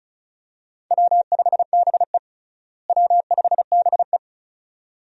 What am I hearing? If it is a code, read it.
Morse code, heard as a keyed tone, about 34 wpm, 700 Hz. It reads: W5BE W5BE
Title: USA Call Signs - 4 Characters - Repeated 2x - 35wpm